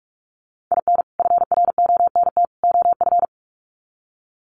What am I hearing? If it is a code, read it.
Morse code, heard as a keyed tone, about 44 wpm, 700 Hz. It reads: IN FRONT OF
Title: Sets of 3 Words 45wpm